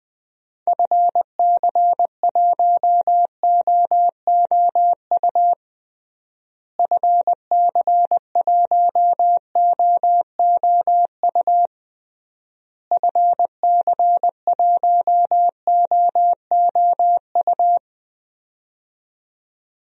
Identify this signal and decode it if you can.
Morse code, heard as a keyed tone, about 20 wpm, 700 Hz. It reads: FC1OOU FC1OOU FC1OOU
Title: DXCC-100 Call Signs - 6 Characters - Repeated 3x - 20wpm